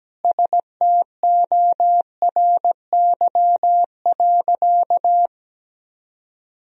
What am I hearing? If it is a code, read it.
Morse code, heard as a keyed tone, about 17 wpm, 700 Hz. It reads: STORY.
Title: Sentences from Top 1000 Words - Review Part 1 - 17wpm